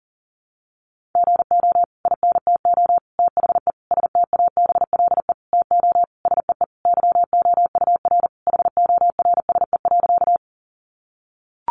Morse code, heard as a keyed tone, about 40 wpm, 700 Hz. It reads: GO INTO THE STABLE TO SEE YOUR HORSE.
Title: Sentences from Top 1500 Words 40wpm